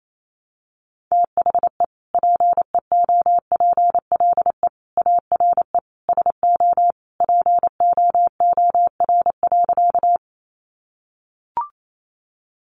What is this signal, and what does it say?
Morse code, heard as a keyed tone, about 28 wpm, 700 Hz. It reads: THE PEOPLE ARE SO POOR.
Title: Sentences from Top 1000 Words 28wpm